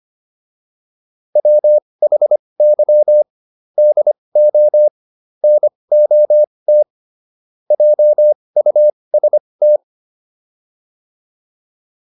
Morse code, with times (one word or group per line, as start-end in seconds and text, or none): 1.4-3.2 WHY
3.8-4.9 DO
5.4-6.8 NOT
7.7-9.8 JUST